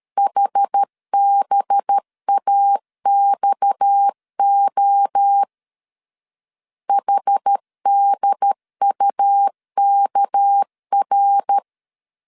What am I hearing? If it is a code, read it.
Morse code, heard as a keyed tone, about 13 wpm, 800 Hz. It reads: HBAXO HDUKR
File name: CW / DET / 1 6 1500C S B 12wpm 3431